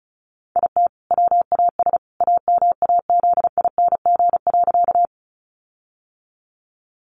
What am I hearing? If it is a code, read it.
Morse code, heard as a keyed tone, about 34 wpm, 700 Hz. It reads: IT WAS AMAZING.
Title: Sentences from Top 1500 Words 35wpm